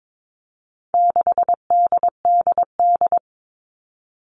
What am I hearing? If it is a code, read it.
Morse code, heard as a keyed tone, about 22 wpm, 700 Hz. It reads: 6DDD
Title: All Cut Numbers - 4-Digit 22wpm